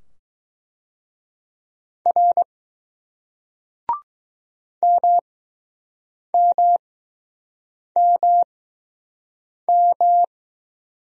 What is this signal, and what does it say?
Morse code, heard as a keyed tone, about 20 wpm, spaced as for 6 wpm, 700 Hz. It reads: R MMMM